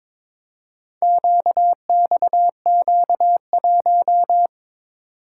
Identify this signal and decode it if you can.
Morse code, heard as a keyed tone, about 22 wpm, 700 Hz. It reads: QXQ1
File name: practice / 22wpm / Four Letter-Number Combinations - 22wpm